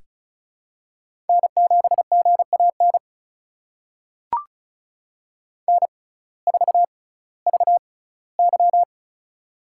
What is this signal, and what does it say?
Morse code, heard as a keyed tone, about 36 wpm, spaced as for 19 wpm, 700 Hz. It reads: N7GAN N4VY
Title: USA Call Signs - 6x Character Spacing - 35wpm